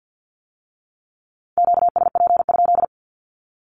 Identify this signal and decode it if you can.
Morse code, heard as a keyed tone, about 50 wpm, 700 Hz. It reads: QSP?
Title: Q-Codes with Spoken Meaning - 50wpm